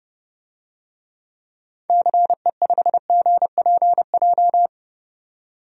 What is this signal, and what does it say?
Morse code, heard as a keyed tone, about 32 wpm, 700 Hz. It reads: CE5GPJ
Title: DXCC-100 Call Signs - 6 Characters - 30wpm